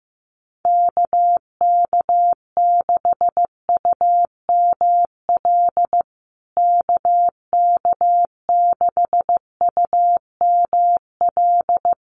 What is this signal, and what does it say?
Morse code, heard as a keyed tone, about 15 wpm, 700 Hz. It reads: KK6UML KK6UML
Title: USA Call Signs Repeated 3x 15wpm